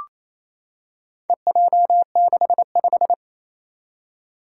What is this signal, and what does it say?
Morse code, heard as a keyed tone, about 28 wpm, 700 Hz. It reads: EJ65
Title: Four Challenging Letter-Number Combinations 28wpm